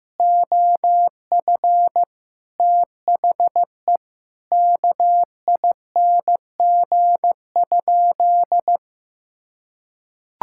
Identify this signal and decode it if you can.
Morse code, heard as a keyed tone, about 15 wpm, 700 Hz. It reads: OF THE KING?
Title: Sentences from Top 1000 Words - Review Part 6 - 15wpm